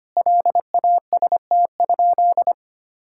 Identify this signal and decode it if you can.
Morse code, heard as a keyed tone, about 25 wpm, 700 Hz. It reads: LAST?
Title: Sentences from Top 1000 Words - Review Part 5 - 25wpm